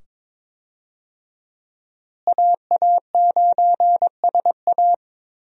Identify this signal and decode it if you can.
Morse code, heard as a keyed tone, about 22 wpm, 700 Hz. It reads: AA9SA
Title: USA Call Signs - 5 Characters - Repeated 2x - 22wpm